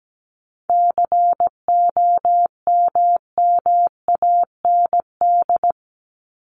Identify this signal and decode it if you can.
Morse code, heard as a keyed tone, about 17 wpm, 700 Hz. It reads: COMMAND
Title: Top 1501-1750 Words 17wpm